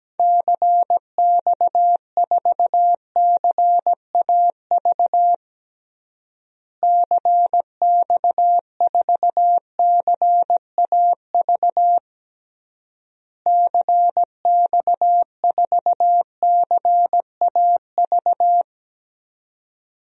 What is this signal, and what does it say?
Morse code, heard as a keyed tone, about 17 wpm, 700 Hz. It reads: CX4CAV CX4CAV CX4CAV
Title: DXCC-100 Call Signs - 6 Characters - Repeated 3x - 17wpm